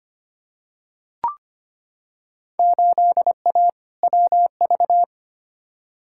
Morse code, heard as a keyed tone, about 25 wpm, 700 Hz. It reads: 8A WV
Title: Field Day Exchanges Encore - 25wpm